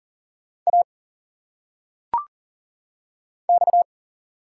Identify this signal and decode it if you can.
Morse code, heard as a keyed tone, about 40 wpm, 700 Hz. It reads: A X
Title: Single Letters - Phonetic Pronunciation 40wpm